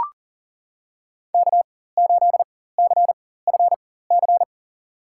Morse code, heard as a keyed tone, about 40 wpm, spaced as for 19 wpm, 700 Hz. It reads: K8CFC
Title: USA Call Signs - 5 Characters - 4x Character Spacing - 40wpm